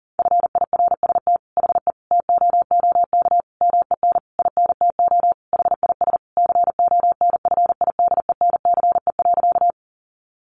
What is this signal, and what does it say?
Morse code, heard as a keyed tone, about 40 wpm, 700 Hz. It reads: FIRST HE TOOK MEN INTO HIS CONFIDENCE.
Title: Sentences from Top 1500 Words 40wpm